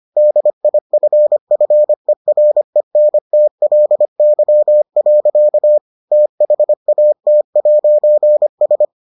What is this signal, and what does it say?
Morse code, heard as a keyed tone, about 25 wpm, 600 Hz. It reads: DIFFERENTLY. THAT'S